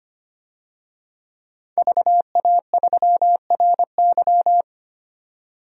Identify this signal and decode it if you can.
Morse code, heard as a keyed tone, about 25 wpm, 700 Hz. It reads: VA3RY
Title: DXCC-100 Call Signs - 5 Characters - 25wpm